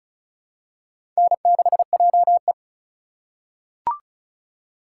Morse code, heard as a keyed tone, about 36 wpm, 700 Hz. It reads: N6JE
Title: USA Call Signs - 4 Characters - 4x Character Spacing - 35wpm